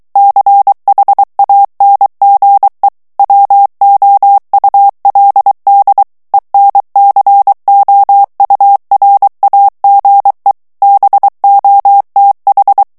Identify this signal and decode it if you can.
Morse code, heard as a keyed tone, about 23 wpm, 800 Hz. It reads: CHANGE WOULD ENCOURAGE BOTH
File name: ANGLAIS texte clair 3 vitesse 22 mots minute